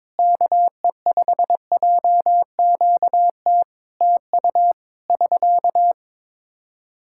Morse code, heard as a keyed tone, about 22 wpm, 700 Hz. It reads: KE5JQT TU <SK>
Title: Practice QSO - Exchange 4 - Ending - 22wpm